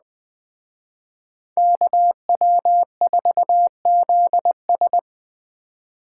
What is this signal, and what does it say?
Morse code, heard as a keyed tone, about 20 wpm, 700 Hz. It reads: KW4ZS
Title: Speed Racing - USA Call Signs 17wpm